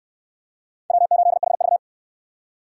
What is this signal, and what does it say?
Morse code, heard as a keyed tone, about 63 wpm, 700 Hz. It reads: W8HV